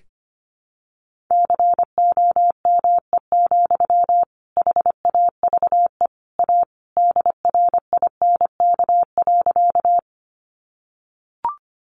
Morse code, heard as a keyed tone, about 25 wpm, 700 Hz. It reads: COME, HAVE A DRINK.
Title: Sentences from Top 1500 Words 25wpm